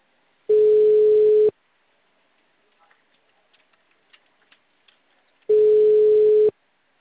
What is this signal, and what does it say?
Ringback tone (Europe, 425 Hz).
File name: out- -2015- - - .48967